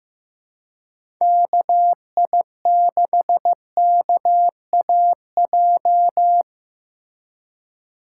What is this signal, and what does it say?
Morse code, heard as a keyed tone, about 15 wpm, 700 Hz. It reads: KI6KAJ
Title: USA Call Signs - 3x Character Spacing - 15wpm